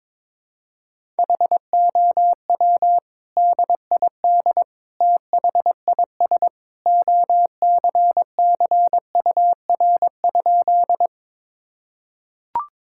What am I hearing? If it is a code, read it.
Morse code, heard as a keyed tone, about 22 wpm, 700 Hz. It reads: HOW DID THIS OCCUR?
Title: Sentences from Top 1000 Words - Review Part 6 - 22wpm